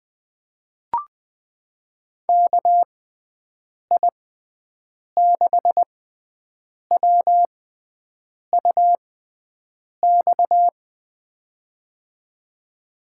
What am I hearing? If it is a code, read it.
Morse code, heard as a keyed tone, about 21 wpm, spaced as for 7 wpm, 700 Hz. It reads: KI6WUX
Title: USA Call Signs - 6 Characters - 6x Character Spacing - 20wpm